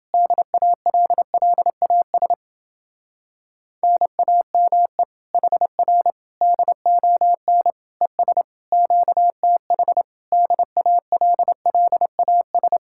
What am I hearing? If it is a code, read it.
Morse code, heard as a keyed tone, about 30 wpm, 700 Hz. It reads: DALLAS NAME HR DON ES QTH DALLAS